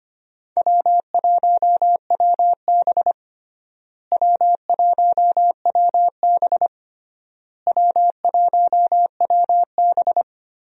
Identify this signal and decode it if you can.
Morse code, heard as a keyed tone, about 25 wpm, 700 Hz. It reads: W1WB W1WB W1WB
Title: USA Call Signs - 4 Characters - Repeated 3x - 25wpm